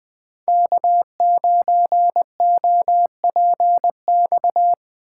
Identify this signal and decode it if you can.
Morse code, heard as a keyed tone, about 20 wpm, 700 Hz. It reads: K9OPX